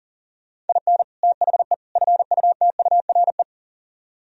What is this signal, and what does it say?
Morse code, heard as a keyed tone, about 51 wpm, 700 Hz. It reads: IN THE FUTURE